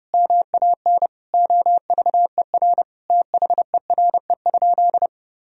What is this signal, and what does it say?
Morse code, heard as a keyed tone, about 32 wpm, 700 Hz. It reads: MAN OVER THERE?
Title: Sentences from Top 200 Words and Abbreviations 30wpm